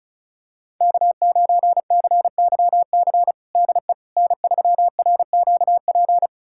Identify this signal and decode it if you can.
Morse code, heard as a keyed tone, about 36 wpm, 700 Hz. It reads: K9CYC DE N3RQP
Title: USA Callsign DE Callsign - 5 Characters - 35wpm